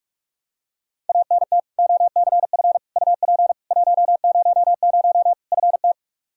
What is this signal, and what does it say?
Morse code, heard as a keyed tone, about 42 wpm, 700 Hz. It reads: ANT OCF UP 100 FT
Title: K1USN SST QSO Practice 45wpm